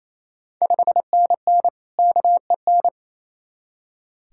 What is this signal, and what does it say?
Morse code, heard as a keyed tone, about 27 wpm, 690 Hz. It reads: 5NN KEN
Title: Ontario QSO Party Exchange 28wpm